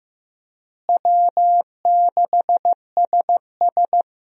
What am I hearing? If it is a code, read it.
Morse code, heard as a keyed tone, about 15 wpm, 700 Hz. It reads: W6SS
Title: USA Call Signs - 4 Characters - Repeated 2x - 15wpm